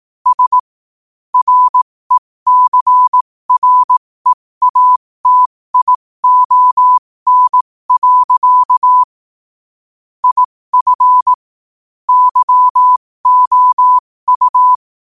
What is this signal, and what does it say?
Morse code, heard as a keyed tone, about 18 wpm, spaced as for 15 wpm, 1000 Hz. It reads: S RECREATION. IF YOU
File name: qotdfiles / 15wpm